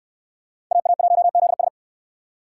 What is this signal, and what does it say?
Morse code, heard as a keyed tone, about 64 wpm, 700 Hz. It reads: AA1ZS